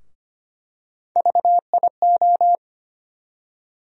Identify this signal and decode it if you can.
Morse code, heard as a keyed tone, about 25 wpm, 700 Hz. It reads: VIO
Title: USA Call Sign Postfixes - 3 Letters 25wpm